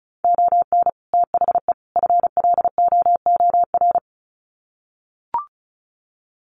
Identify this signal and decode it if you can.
Morse code, heard as a keyed tone, about 34 wpm, 700 Hz. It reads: ON THE FLOOR
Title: Sets of 3 Words 35wpm